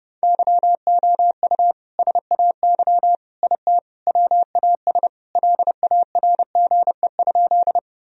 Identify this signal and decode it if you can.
Morse code, heard as a keyed tone, about 30 wpm, 700 Hz. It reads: YOU SAY IT WAS LARGE?
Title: Sentences from Top 1000 Words - Review Part 1 - 30wpm